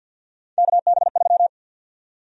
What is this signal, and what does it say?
Morse code, heard as a keyed tone, about 50 wpm, 700 Hz. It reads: KB3
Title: USA Call Sign Prefixes - Repeated 2x - 50wpm